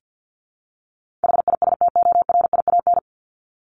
Morse code, heard as a keyed tone, about 50 wpm, 700 Hz. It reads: HISTORIAN
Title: Top 1751-2000 Words 50wpm